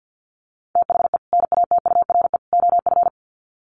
Morse code, heard as a keyed tone, about 50 wpm, 700 Hz. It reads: THE NATURE OF
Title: Sets of 3 Words 50wpm